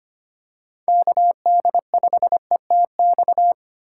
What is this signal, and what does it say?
Morse code, heard as a keyed tone, about 25 wpm, 700 Hz. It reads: KD5ETX
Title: USA Call Signs - 6 Characters - 3x Character Spacing - 25wpm